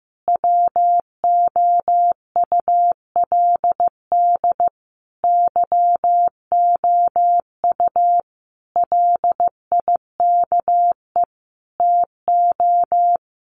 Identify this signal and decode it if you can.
Morse code, heard as a keyed tone, about 15 wpm, 700 Hz. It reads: WOULD YOU LIKE TO
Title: Sentences from Top 100 Words 15wpm